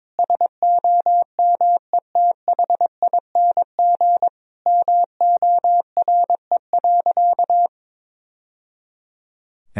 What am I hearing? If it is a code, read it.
Morse code, heard as a keyed tone, about 22 wpm, 700 Hz. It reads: SOMETHING MORE.
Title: Sentences from Top 1000 Words - Review Part 1 - 22wpm